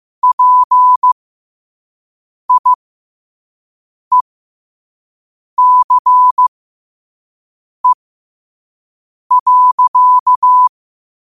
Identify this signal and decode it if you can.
Morse code, heard as a keyed tone, about 15 wpm, spaced as for 5 wpm, 1000 Hz. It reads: PIECE.
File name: qotdfiles / 5wpm